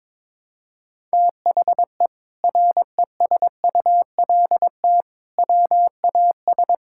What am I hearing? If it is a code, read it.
Morse code, heard as a keyed tone, about 22 wpm, 700 Hz. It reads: THE RESULT WAS
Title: Sentences from Top 1000 Words - Review Part 5 - 22wpm